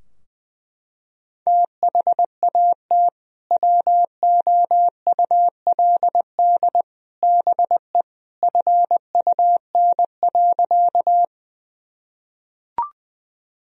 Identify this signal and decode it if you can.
Morse code, heard as a keyed tone, about 19 wpm, 700 Hz. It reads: THAT WOULD BE FUN.
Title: Sentences from Top 1000 Words - Review Part 4 - 20wpm